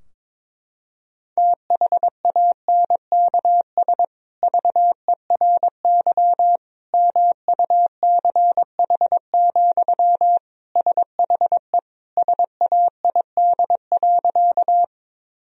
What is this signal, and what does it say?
Morse code, heard as a keyed tone, about 22 wpm, 700 Hz. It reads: THANKS VERY MUCH, SHE SAID.
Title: Sentences from Top 1000 Words - Review Part 3 - 22wpm